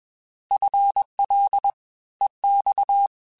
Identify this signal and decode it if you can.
Morse code, heard as a keyed tone, about 21 wpm, 800 Hz. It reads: FL EX
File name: FL / FL mp3s / Fast FL @20